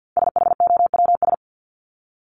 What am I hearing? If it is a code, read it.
Morse code, heard as a keyed tone, about 50 wpm, 700 Hz. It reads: SHOWS
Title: ES:S Suffix - Top 100 Words with Base Words 50wpm